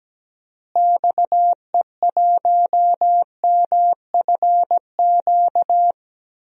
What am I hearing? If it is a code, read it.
Morse code, heard as a keyed tone, about 17 wpm, 700 Hz. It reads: XE1MFQ
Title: DXCC-100 Call Signs - 6 Characters - Repeated 2x - 17wpm